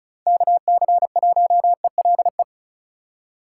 Morse code, heard as a keyed tone, about 36 wpm, 700 Hz. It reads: KC1ELE
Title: USA Call Signs - 6 Characters - 3x Character Spacing - 35wpm